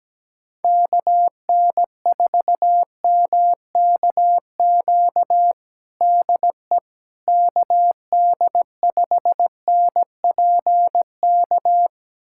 Morse code, heard as a keyed tone, about 17 wpm, 700 Hz. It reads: KN4MKQ DE KD5NPK
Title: USA Callsign DE Callsign - 6 Characters - 17wpm